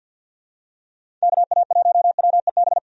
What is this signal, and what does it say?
Morse code, heard as a keyed tone, about 50 wpm, 700 Hz. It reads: KA1WEB
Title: USA Call Signs - 6 Characters - 50wpm